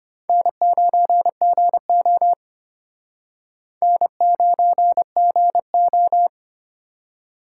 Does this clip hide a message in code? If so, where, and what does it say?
Yes: Morse code from 0.3 to 6.3 s, heard as a keyed tone, about 32 wpm, 700 Hz. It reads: N9GO N9GO